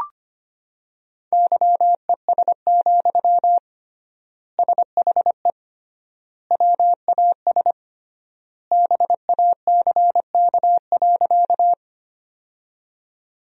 Morse code, heard as a keyed tone, about 25 wpm, 700 Hz. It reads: YES, SHE WAS BACK.
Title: 3x Word-Spacing - Sentences from Top 500 Words - 25wpm